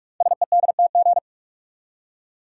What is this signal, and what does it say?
Morse code, heard as a keyed tone, about 44 wpm, 700 Hz. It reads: SEDTG
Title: Code Groups - 5 Letter:Numbers - 45wpm